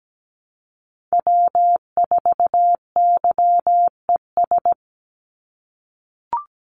Morse code, heard as a keyed tone, about 17 wpm, 700 Hz. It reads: W4YES
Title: Call Signs - 17wpm